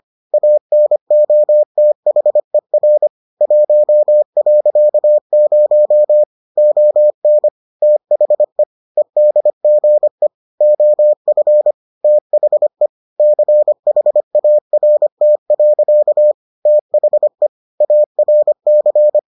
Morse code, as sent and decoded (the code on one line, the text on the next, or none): .- -. --- - .... .-. / .---- .-.-.- ----- / --- -. / - .... / . -.. --. / --- ..-. / - .... / -.-. .... .- .-. - .-.-.- / - .... / .- .-. -.-.
ANOTHER 1.0 ON THE EDGE OF THE CHART. THE ARC